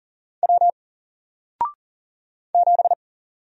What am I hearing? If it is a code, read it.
Morse code, heard as a keyed tone, about 38 wpm, 700 Hz. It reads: W 7
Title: Single Character (Letter:Number:Punctuation:Prosign) Alt 40wpm